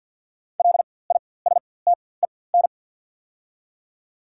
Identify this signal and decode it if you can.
Morse code, heard as a keyed tone, about 48 wpm, spaced as for 23 wpm, 700 Hz. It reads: LISTEN
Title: 4x Character Spacing - Top 500 Words - 50wpm